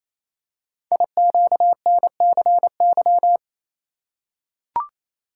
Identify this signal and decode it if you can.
Morse code, heard as a keyed tone, about 28 wpm, 700 Hz. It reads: IQNCY
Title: Code Groups - 5 Letters - 28wpm